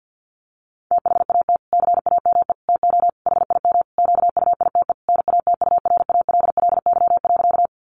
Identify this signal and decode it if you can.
Morse code, heard as a keyed tone, about 48 wpm, 700 Hz. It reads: THAT CAME TO HIM QUITE NATURALLY.
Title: Sentences from Top 1300 Words 50wpm